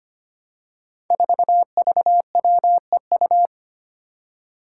Morse code, heard as a keyed tone, about 25 wpm, 700 Hz. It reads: 4VWEU
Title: Five Challenging Letter-Number Combinations 25wpm